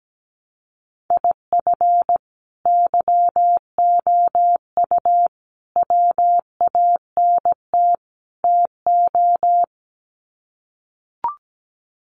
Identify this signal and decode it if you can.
Morse code, heard as a keyed tone, about 17 wpm, 700 Hz. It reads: IF YOU WANT TO
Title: Sets of 4 Words 17wpm